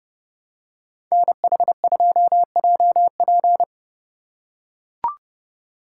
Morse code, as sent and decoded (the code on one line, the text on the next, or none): -. .... ..--- .--- .--.
NH2JP